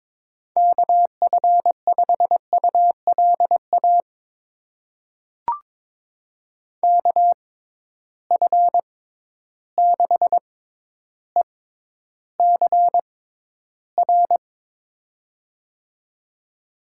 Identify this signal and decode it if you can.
Morse code, heard as a keyed tone, about 22 wpm, spaced as for 11 wpm, 700 Hz. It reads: KF5ULA KF6ECR
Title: USA Call Signs - 6 Characters - 6x Character Spacing - 22wpm